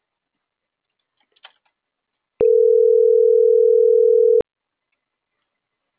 Ringback tone (North America), 440 + 480 Hz.